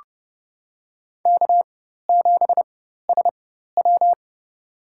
Morse code, heard as a keyed tone, about 31 wpm, spaced as for 14 wpm, 700 Hz. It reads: K7SW
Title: USA Call Signs - 4x Character Spacing - 30wpm